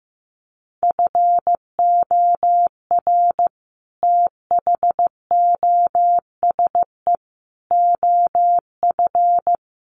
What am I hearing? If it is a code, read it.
Morse code, heard as a keyed tone, about 15 wpm, 700 Hz. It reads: FOR THOSE OF
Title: Sets of 5 Words Encore 15wpm